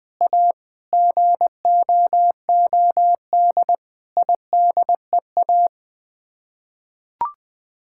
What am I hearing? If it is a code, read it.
Morse code, heard as a keyed tone, about 20 wpm, 700 Hz. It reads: A GOOD IDEA
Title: Sets of 6 Words - 20wpm